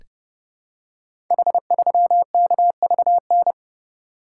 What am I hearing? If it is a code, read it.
Morse code, heard as a keyed tone, about 29 wpm, 700 Hz. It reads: H3KVN
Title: Code Groups - 5 Letter:Numbers - 30wpm